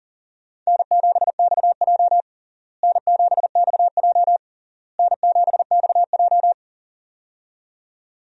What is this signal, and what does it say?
Morse code, heard as a keyed tone, about 40 wpm, 700 Hz. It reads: N7XJ N7XJ N7XJ
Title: USA Call Signs - 4 Characters - Repeated 3x - 40wpm